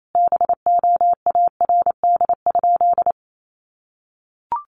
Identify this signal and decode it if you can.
Morse code, heard as a keyed tone, about 28 wpm, 700 Hz. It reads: BOARD?
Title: Sentences from Top 1300 Words 28wpm